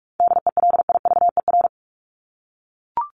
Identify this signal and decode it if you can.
Morse code, heard as a keyed tone, about 44 wpm, 700 Hz. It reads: DELIVER
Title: Top 1751-2000 Words 45wpm